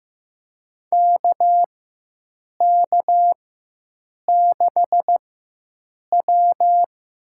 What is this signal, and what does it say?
Morse code, heard as a keyed tone, about 15 wpm, spaced as for 7 wpm, 700 Hz. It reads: KK6W
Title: USA Call Signs - 6 Characters - 4x Character Spacing - 15wpm